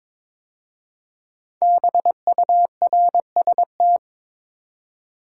Morse code, heard as a keyed tone, about 22 wpm, 700 Hz. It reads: BURST
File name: 5 Letter Words - First 3 Letters Given - 22wpm